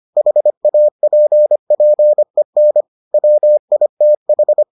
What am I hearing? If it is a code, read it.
Morse code, heard as a keyed tone, about 25 wpm, 600 Hz. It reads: HAPPEN WITH